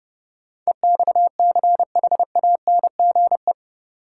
Morse code, heard as a keyed tone, about 30 wpm, 700 Hz. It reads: EXCHANGE
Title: Sentences from Top 1000 Words - Review Part 6 - New Word Drill - 30wpm